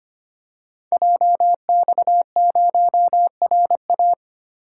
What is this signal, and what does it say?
Morse code, heard as a keyed tone, about 25 wpm, 700 Hz. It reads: JX0RA
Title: Code Groups - 5 Letter:Numbers - 25wpm